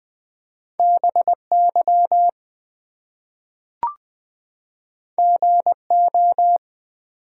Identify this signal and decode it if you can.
Morse code, heard as a keyed tone, about 20 wpm, 700 Hz. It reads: BY GO
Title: Two Letter Words - 20wpm